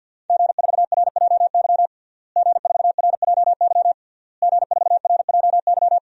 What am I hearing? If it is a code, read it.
Morse code, heard as a keyed tone, about 51 wpm, 700 Hz. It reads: G4RJY G4RJY G4RJY
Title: DXCC-100 Call Signs - 5 Characters - Repeated 3x - 50wpm